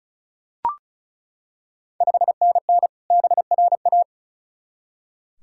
Morse code, heard as a keyed tone, about 36 wpm, 690 Hz. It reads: 5NN BRA
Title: Ontario QSO Party Exchange 35wpm